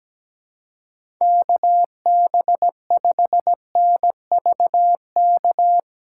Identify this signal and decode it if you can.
Morse code, heard as a keyed tone, about 17 wpm, 700 Hz. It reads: KB5NVK
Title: USA Callsign DE Callsign 17wpm